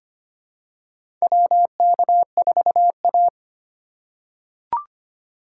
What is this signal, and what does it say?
Morse code, heard as a keyed tone, about 25 wpm, 700 Hz. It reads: WK4A
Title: USA Call Signs - 4 Characters - 2x Character Spacing - 25wpm